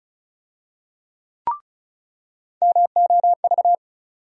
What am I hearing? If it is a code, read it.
Morse code, heard as a keyed tone, about 36 wpm, 700 Hz. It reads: MOV
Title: USA Call Sign Postfixes - 3 Letters 35wpm